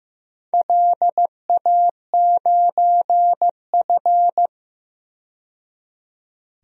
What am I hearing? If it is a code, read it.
Morse code, heard as a keyed tone, about 15 wpm, 700 Hz. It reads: LA9F
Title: DXCC-100 Call Signs - 4 Characters - 15wpm